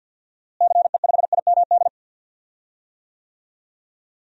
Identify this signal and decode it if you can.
Morse code, heard as a keyed tone, about 51 wpm, 700 Hz. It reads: CE5IMD
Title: DXCC-100 Call Signs - 6 Characters - Repeated 2x - 50wpm